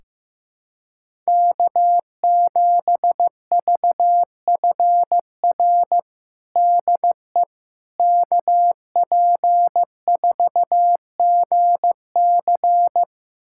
Morse code, heard as a keyed tone, about 15 wpm, 700 Hz. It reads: K7VFR DE KP4GC
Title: USA Callsign DE Callsign - 5 Characters - 15wpm